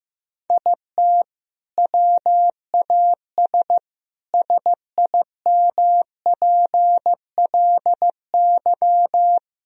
Morse code, heard as a keyed tone, about 15 wpm, 700 Hz. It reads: IT WAS SIMPLY
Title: Sentences from Top 1000 Words - Review Part 4 - 15wpm